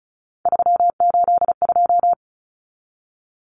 Morse code, heard as a keyed tone, about 34 wpm, 700 Hz. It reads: 382
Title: Common Cut Numbers - 3-Digit 35wpm